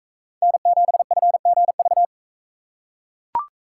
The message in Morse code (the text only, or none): N7FGV